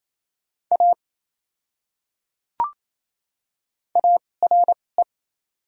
Morse code, heard as a keyed tone, about 28 wpm, spaced as for 20 wpm, 700 Hz. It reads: A ARE
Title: 2x Character Spacing - Top 200 Words - 28wpm